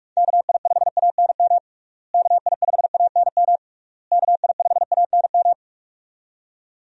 Morse code, heard as a keyed tone, about 46 wpm, 700 Hz. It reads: KI5ANM KI5ANM KI5ANM
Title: USA Call Signs - 6 Characters - Repeated 3x - 45wpm